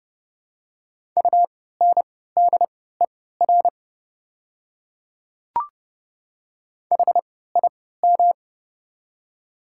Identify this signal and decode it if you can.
Morse code, heard as a keyed tone, about 30 wpm, spaced as for 17 wpm, 690 Hz. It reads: UNDER HIM
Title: 3x Character Spacing - Top 200 Words - 30wpm